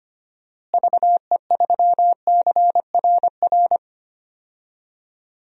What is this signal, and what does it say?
Morse code, heard as a keyed tone, about 25 wpm, 700 Hz. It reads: VE3CRR
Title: DXCC-100 Call Signs - 6 Characters - Repeated 3x - 25wpm